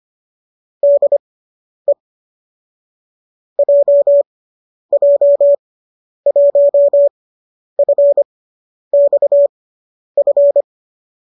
Morse code, heard as a keyed tone, about 25 wpm, spaced as for 10 wpm, 570 Hz. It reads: DE JJ1FXF
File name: qso005 10